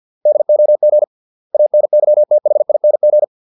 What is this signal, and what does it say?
Morse code, heard as a keyed tone, about 50 wpm, 600 Hz. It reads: DOG ANYTHING
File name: 632 words / 632 Words 50WPM